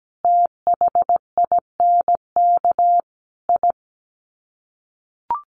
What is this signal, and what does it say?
Morse code, heard as a keyed tone, about 17 wpm, 700 Hz. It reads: THINK I
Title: Sets of 5 Words Encore 17wpm